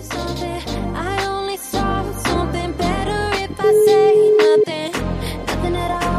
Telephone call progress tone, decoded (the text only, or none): ringback tone